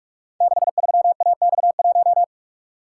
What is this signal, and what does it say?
Morse code, heard as a keyed tone, about 44 wpm, 700 Hz. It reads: 63AX1
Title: Code Groups - 5 Letter:Numbers - 45wpm